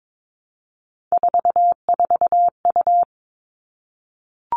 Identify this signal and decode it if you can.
Morse code, heard as a keyed tone, about 22 wpm, 700 Hz. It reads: 44U
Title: All Cut Numbers - 3-Digit 22wpm